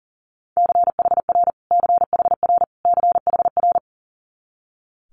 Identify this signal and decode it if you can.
Morse code, heard as a keyed tone, about 42 wpm, 690 Hz. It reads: CHR CHR CHR
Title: Multi-letter Phonemes Repeated 3x 40wpm